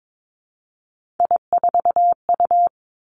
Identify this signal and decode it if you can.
Morse code, heard as a keyed tone, about 22 wpm, 700 Hz. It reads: I4U
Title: Three Challenging Letter-Number Combinations 22wpm